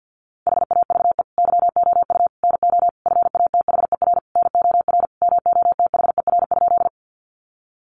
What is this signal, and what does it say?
Morse code, heard as a keyed tone, about 50 wpm, 700 Hz. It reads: HAVE YOU NO FATHER NOR MOTHER?
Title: Sentences from Top 1500 Words 50wpm